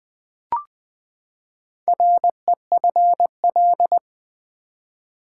Morse code, heard as a keyed tone, about 20 wpm, 700 Hz. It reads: REFL
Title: Build A Word - 4 Letters At A Time - 10-Letter Words - 20wpm